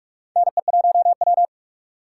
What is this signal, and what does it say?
Morse code, heard as a keyed tone, about 46 wpm, 700 Hz. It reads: NE1W
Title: USA Call Signs - 4 Characters - 4x Character Spacing - 45wpm